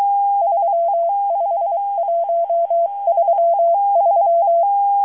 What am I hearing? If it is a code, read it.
Morse code, heard as a keyed tone, about 23 wpm, spaced as for 21 wpm, 680 Hz. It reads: 35133
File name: HM02- -0437utc-7351khz